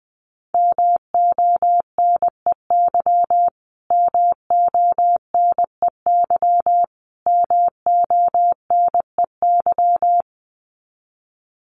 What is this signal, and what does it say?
Morse code, heard as a keyed tone, about 20 wpm, 700 Hz. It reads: MONEY MONEY MONEY
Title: Top 500 Words Repeated 20wpm